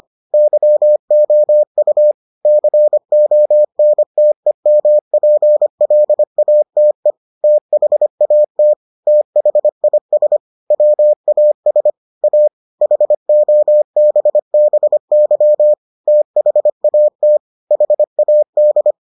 Morse code, heard as a keyed tone, about 25 wpm, 600 Hz. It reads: YOU CONTEMPLATE THAT THIS WAS A HOBBY THAT HAD